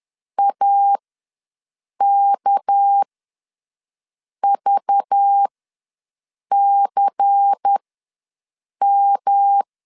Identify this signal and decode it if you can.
Morse code, heard as a keyed tone, about 10 wpm, spaced as for 6 wpm, 800 Hz. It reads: AKVCM
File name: LEZIONI / 6 PKX0 / 6 PK ws 10wpm 3745